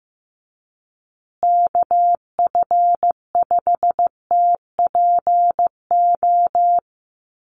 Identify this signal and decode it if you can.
Morse code, heard as a keyed tone, about 15 wpm, 700 Hz. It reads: KF5TPO
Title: USA Call Signs Repeated 2x 15wpm